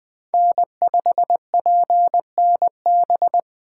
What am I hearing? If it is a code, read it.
Morse code, heard as a keyed tone, about 20 wpm, 700 Hz. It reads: N5PNB
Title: Speed Racing - USA 1x3 Call Signs 17wpm